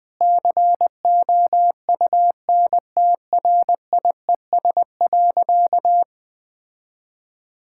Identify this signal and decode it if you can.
Morse code, heard as a keyed tone, about 21 wpm, 700 Hz. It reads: COUNTRIES.
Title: Difficult Sentences with Hints - 20wpm